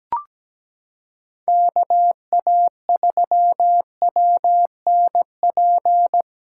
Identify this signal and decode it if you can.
Morse code, heard as a keyed tone, about 17 wpm, 700 Hz. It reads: KA3WNP
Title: USA Callsign DE Callsign - 6 Characters - 17wpm